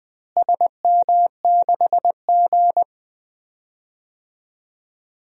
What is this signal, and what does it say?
Morse code, heard as a keyed tone, about 21 wpm, 700 Hz. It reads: SM6G
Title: DXCC-100 Call Signs - 4 Characters - Repeated 2x - 20wpm